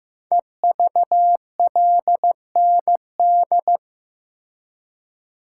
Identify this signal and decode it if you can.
Morse code, heard as a keyed tone, about 15 wpm, 700 Hz. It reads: EVLND
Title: Code Groups - 5 Letter:Numbers - 15wpm